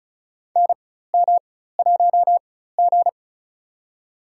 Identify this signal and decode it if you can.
Morse code, heard as a keyed tone, about 36 wpm, spaced as for 16 wpm, 700 Hz. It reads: NM1G
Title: USA Call Signs - 4 Characters - 4x Character Spacing - 35wpm